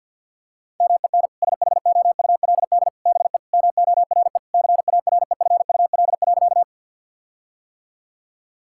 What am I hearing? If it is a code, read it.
Morse code, heard as a keyed tone, about 50 wpm, 700 Hz. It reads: MEN SHOULD BE MORE CAREFUL.
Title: Sentences from Top 1600 Words 50wpm